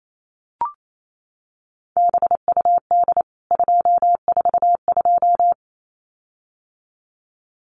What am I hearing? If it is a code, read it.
Morse code, heard as a keyed tone, about 28 wpm, 700 Hz. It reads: BUD 242
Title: CWT Exchange 28wpm